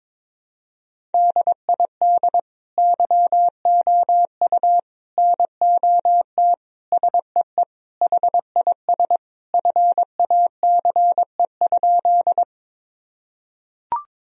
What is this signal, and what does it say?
Morse code, heard as a keyed tone, about 22 wpm, 700 Hz. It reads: DID YOU NOT SEE HIS FACE?
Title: Sentences from Top 1000 Words - Review Part 2 - 22wpm